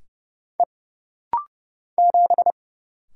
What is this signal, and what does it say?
Morse code, heard as a keyed tone, about 30 wpm, 700 Hz. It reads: E 7
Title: Single Letter-Number Alt - Rapid-Fire 30wpm